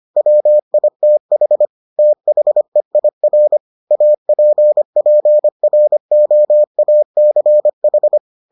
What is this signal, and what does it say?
Morse code, heard as a keyed tone, about 25 wpm, 600 Hz. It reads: WITH THEIR APPROACH